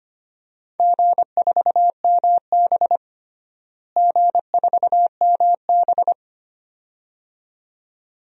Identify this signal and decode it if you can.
Morse code, heard as a keyed tone, about 25 wpm, 700 Hz. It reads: G4MB G4MB
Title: DXCC-100 Call Signs - 4 Characters - Repeated 2x - 25wpm